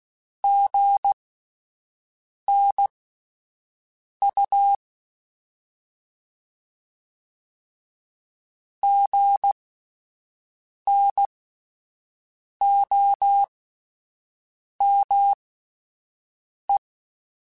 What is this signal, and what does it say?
Morse code, heard as a keyed tone, about 16 wpm, spaced as for 5 wpm, 800 Hz. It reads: GNU GNOME